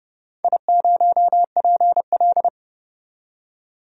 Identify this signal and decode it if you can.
Morse code, heard as a keyed tone, about 30 wpm, 700 Hz. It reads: I0PL
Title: DXCC-100 Call Signs - 4 Characters - 30wpm